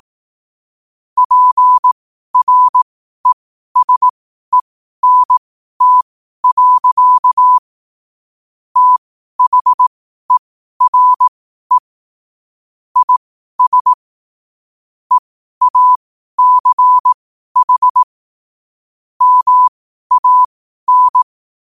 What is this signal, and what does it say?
Morse code, heard as a keyed tone, about 18 wpm, spaced as for 12 wpm, 1000 Hz. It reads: PRESENT. THERE IS EACH MAN
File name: qotdfiles / 12wpm